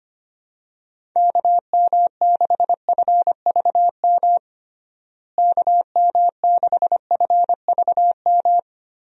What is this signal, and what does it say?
Morse code, heard as a keyed tone, about 25 wpm, 700 Hz. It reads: KM6FVM KM6FVM
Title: USA Call Signs - 6 Characters - Repeated 2x - 25wpm